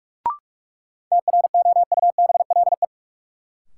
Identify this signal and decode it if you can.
Morse code, heard as a keyed tone, about 46 wpm, 700 Hz. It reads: TROUBLE
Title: Sentences from Top 1200 Words - New Word Drill Alt - 45wpm